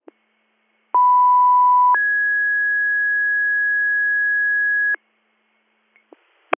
Two-tone page, heard as a one-second tone, then a three-second tone, about 990 then 1650 Hz. It is heard from 0.9 to 5.0 s.
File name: 100- .0-call 314